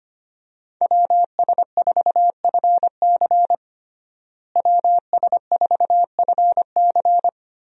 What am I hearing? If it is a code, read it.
Morse code, heard as a keyed tone, about 25 wpm, 700 Hz. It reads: WS4FC WS4FC
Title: USA Call Signs - 5 Characters - Repeated 3x - 25wpm